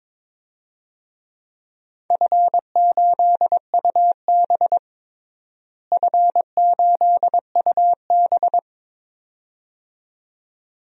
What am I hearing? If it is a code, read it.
Morse code, heard as a keyed tone, about 22 wpm, 700 Hz. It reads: F8UB F8UB
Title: DXCC-100 Call Signs - 4 Characters - Repeated 2x - 22wpm